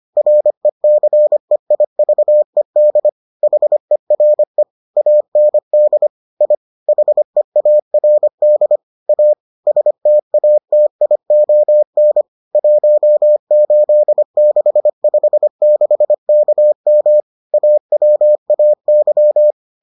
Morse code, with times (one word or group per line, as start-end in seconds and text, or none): 0.2-3.1 RECEIVED
3.4-4.6 HERE
5.0-6.1 AND
6.4-6.5 I
6.9-8.8 HEARD
9.1-9.3 A
9.7-12.2 STATION
12.6-17.2 18656KM
17.5-19.5 AWAY